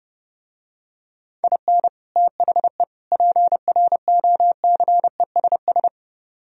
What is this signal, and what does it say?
Morse code, heard as a keyed tone, about 30 wpm, 700 Hz. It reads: IN THE PROCESS